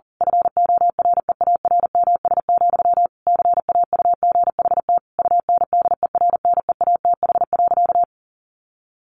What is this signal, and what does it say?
Morse code, heard as a keyed tone, about 40 wpm, 700 Hz. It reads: FOREARMS, CAUGHT UNDERNEATH.
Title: A Sound Of Thunder 40wpm